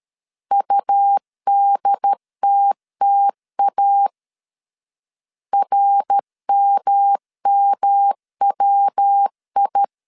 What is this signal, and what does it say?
Morse code, heard as a keyed tone, about 13 wpm, 800 Hz. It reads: UDTTA RMMWI
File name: CW / DET / 1 3 1500C S B 12wpm 2828